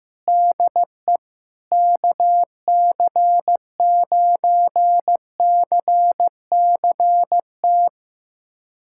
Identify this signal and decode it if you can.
Morse code, heard as a keyed tone, about 15 wpm, 700 Hz. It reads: DE KC9CCT
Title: USA Callsign DE Callsign - 6 Characters - 15wpm